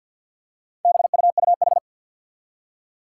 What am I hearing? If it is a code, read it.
Morse code, heard as a keyed tone, about 48 wpm, 700 Hz. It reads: BUUH
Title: Four Challenging Letter-Number Combinations 50wpm